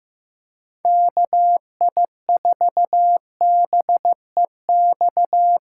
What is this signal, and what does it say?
Morse code, heard as a keyed tone, about 15 wpm, 700 Hz. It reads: KI4BEX
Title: USA Call Signs - 6 Characters - 3x Character Spacing - 15wpm